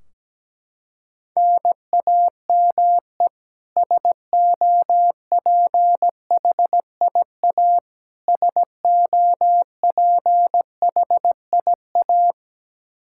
Morse code, heard as a keyed tone, about 17 wpm, 700 Hz. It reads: NAME SOPHIA SOPHIA
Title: Practice QSO - Exchange 1 - Beginning - 17wpm